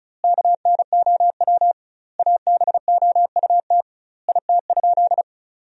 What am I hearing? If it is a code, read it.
Morse code, heard as a keyed tone, about 34 wpm, 700 Hz. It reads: KNOW ABOUT IT?
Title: 2x Word-Spacing - Sentences from Top 100 Words - 35wpm